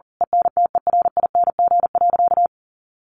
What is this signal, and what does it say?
Morse code, heard as a keyed tone, about 40 wpm, 700 Hz. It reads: ENTERING.
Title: A Sound Of Thunder 40wpm